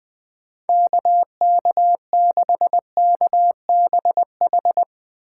Morse code, heard as a keyed tone, about 20 wpm, 700 Hz. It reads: KK6KBH